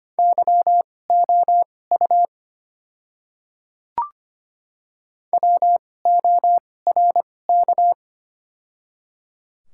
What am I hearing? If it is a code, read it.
Morse code, heard as a keyed tone, about 25 wpm, spaced as for 18 wpm, 700 Hz. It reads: YOU WORK
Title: 2x Character Spacing - Top 200 Words - 25wpm